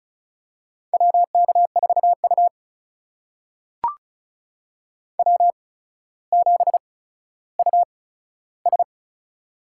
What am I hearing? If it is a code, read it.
Morse code, heard as a keyed tone, about 36 wpm, spaced as for 15 wpm, 700 Hz. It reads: WK4U W7US
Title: USA Call Signs - 4 Characters - 8x Character Spacing - 35wpm